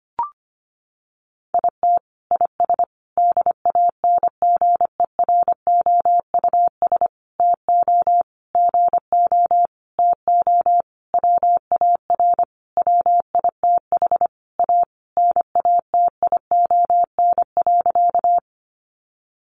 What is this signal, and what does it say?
Morse code, heard as a keyed tone, about 25 wpm, 700 Hz. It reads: IT IS DANGEROUS TO GO TO WAR WITH A NATION.
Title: Sentences from Top 1200 Words 25wpm